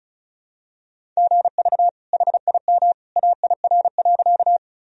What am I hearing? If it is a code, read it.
Morse code, heard as a keyed tone, about 34 wpm, 700 Hz. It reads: GV HIM AIR.
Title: Sentences from Top 200 Words and Abbreviations 35wpm